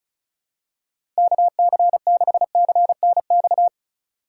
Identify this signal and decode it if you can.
Morse code, heard as a keyed tone, about 36 wpm, 700 Hz. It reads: KC6CNX
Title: USA Call Signs - 6 Characters - 2x Character Spacing - 35wpm